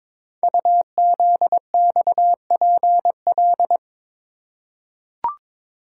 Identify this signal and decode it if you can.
Morse code, heard as a keyed tone, about 22 wpm, 700 Hz. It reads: UZXPL
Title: Code Groups - 5 Letters - 22wpm